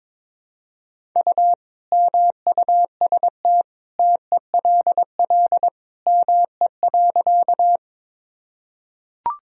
Morse code, heard as a keyed tone, about 22 wpm, 700 Hz. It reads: U MUST TELL ME.
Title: Sentences from Top 200 Words and Abbreviations 22wpm